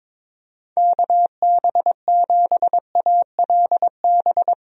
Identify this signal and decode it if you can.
Morse code, heard as a keyed tone, about 21 wpm, 700 Hz. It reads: KB7ALB
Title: USA Call Signs - 6 Characters - 22wpm